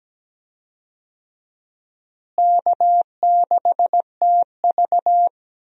Morse code, heard as a keyed tone, about 17 wpm, 700 Hz. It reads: K6TV
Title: USA Call Signs - 4 Characters - Repeated 3x - 17wpm